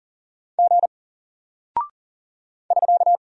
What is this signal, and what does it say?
Morse code, heard as a keyed tone, about 40 wpm, 700 Hz. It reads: G <SK>
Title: Single Character (Letter:Number:Punctuation:Prosign) Alt 40wpm